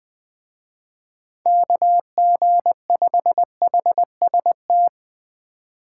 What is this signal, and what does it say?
Morse code, heard as a keyed tone, about 20 wpm, 700 Hz. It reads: KG5HST
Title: Speed Racing - USA Call Signs 17wpm